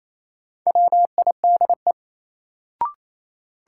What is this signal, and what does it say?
Morse code, heard as a keyed tone, about 27 wpm, 700 Hz. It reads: WIDE
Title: Sets of 4 Words - New Word Drill - 28wpm